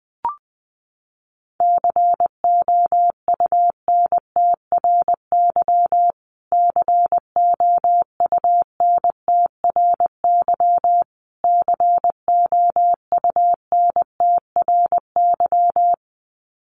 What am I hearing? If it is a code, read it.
Morse code, heard as a keyed tone, about 20 wpm, 700 Hz. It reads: COUNTRY COUNTRY COUNTRY
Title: Top 500 Words Repeated 20wpm